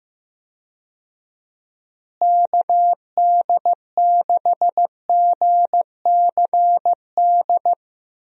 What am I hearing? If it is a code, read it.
Morse code, heard as a keyed tone, about 15 wpm, 700 Hz. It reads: KD6GCD
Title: USA Call Signs - 6 Characters - Repeated 2x - 15wpm